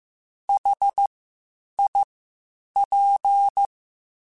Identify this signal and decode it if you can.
Morse code, heard as a keyed tone, about 15 wpm, spaced as for 8 wpm, 800 Hz. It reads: HIP
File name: H Prefixes / H mp3s